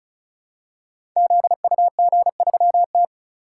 Morse code, heard as a keyed tone, about 36 wpm, 700 Hz. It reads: ZUG3T
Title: Code Groups - 5 Letter:Numbers - 35wpm